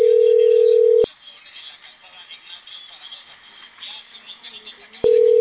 Ringback tone (North America), 440 + 480 Hz.